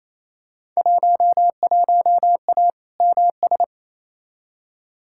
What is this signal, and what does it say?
Morse code, heard as a keyed tone, about 28 wpm, 700 Hz. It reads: 11A MS
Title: Field Day Exchanges Encore - 28wpm